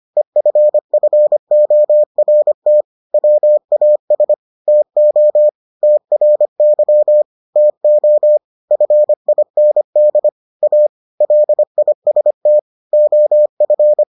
Morse code, heard as a keyed tone, about 25 wpm, 600 Hz. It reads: EFFORT WAS TO TRY TO FIND A LIST OF